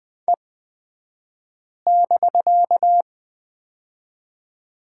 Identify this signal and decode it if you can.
Morse code, heard as a keyed tone, about 20 wpm, spaced as for 5 wpm, 700 Hz. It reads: E<BK>
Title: 249 Lesson 40d - BK - Two Characters 20wpm